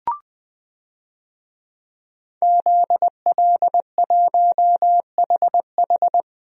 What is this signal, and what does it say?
Morse code, heard as a keyed tone, about 21 wpm, 700 Hz. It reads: ZL1HH
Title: DXCC-100 Call Signs - 5 Characters - 20wpm